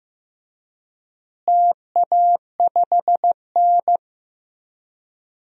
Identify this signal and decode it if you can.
Morse code, heard as a keyed tone, about 15 wpm, 700 Hz. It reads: TA5N
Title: DXCC-100 Call Signs - 4 Characters - Repeated 3x - 15wpm